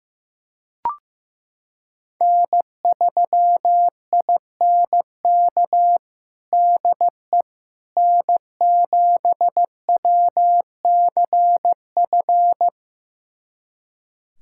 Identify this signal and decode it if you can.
Morse code, heard as a keyed tone, about 15 wpm, 700 Hz. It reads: N3INK DE N7WCF
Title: USA Callsign DE Callsign - 5 Characters - 15wpm